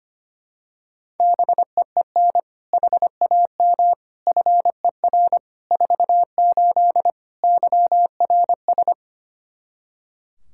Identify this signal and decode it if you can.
Morse code, heard as a keyed tone, about 25 wpm, 700 Hz. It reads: BEEN HAM FER 48 YRS
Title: Practice QSO - Exchange 3 - More Info - 25wpm